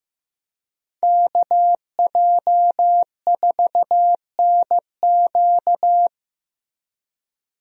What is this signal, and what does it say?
Morse code, heard as a keyed tone, about 15 wpm, 700 Hz. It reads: KJ4NQ
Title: USA Call Signs - 5 Characters - Repeated 3x - 15wpm